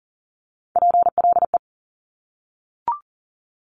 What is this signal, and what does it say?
Morse code, heard as a keyed tone, about 41 wpm, 700 Hz. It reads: PLE
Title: Top Three Letter Combinations in Words 40wpm